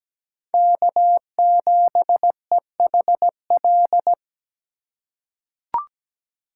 Morse code, heard as a keyed tone, about 17 wpm, 700 Hz. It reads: K7EHL
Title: USA Call Signs - 5 Characters - 3x Character Spacing - 17wpm